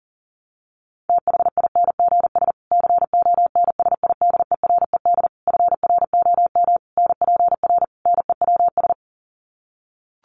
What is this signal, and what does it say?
Morse code, heard as a keyed tone, about 37 wpm, 700 Hz. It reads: THINGS CONSIDERED FROM NPR NEWS
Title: Sets of 5 Words Encore 40wpm